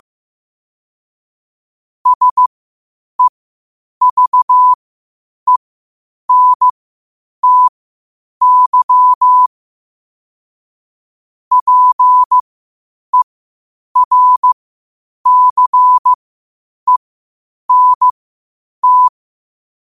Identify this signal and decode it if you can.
Morse code, heard as a keyed tone, about 15 wpm, spaced as for 8 wpm, 1000 Hz. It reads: SEVENTY PERCENT
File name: qotdfiles / 8wpm